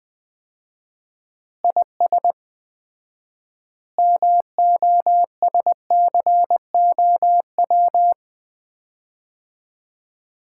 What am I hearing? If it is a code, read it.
Morse code, heard as a keyed tone, about 20 wpm, 700 Hz. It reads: IS MOSCOW
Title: Common QSO Phrases - 4x Word-Spacing - 20wpm